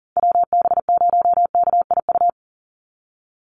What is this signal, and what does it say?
Morse code, heard as a keyed tone, about 40 wpm, 700 Hz. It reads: WB0KIU
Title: NA SOTA Chaser Callsigns 40wpm